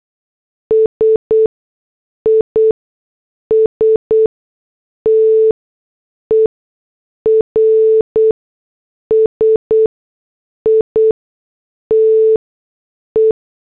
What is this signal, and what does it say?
Morse code, heard as a keyed tone, about 8 wpm, spaced as for 6 wpm, 440 Hz. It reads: SISTER SITE